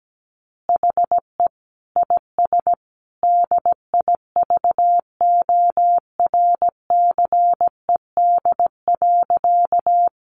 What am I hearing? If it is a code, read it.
Morse code, heard as a keyed tone, about 17 wpm, 700 Hz. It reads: HE IS DIVORCED.
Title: Additional Easy Phrases 17wpm